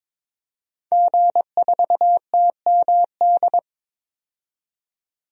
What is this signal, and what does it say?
Morse code, heard as a keyed tone, about 22 wpm, 700 Hz. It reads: G4TMD
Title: DXCC-100 Call Signs - 5 Characters - 22wpm